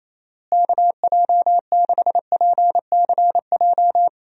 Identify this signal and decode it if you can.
Morse code, heard as a keyed tone, about 28 wpm, 700 Hz. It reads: KJ6PCJ
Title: USA Call Signs - 6 Characters - Repeated 2x - 28wpm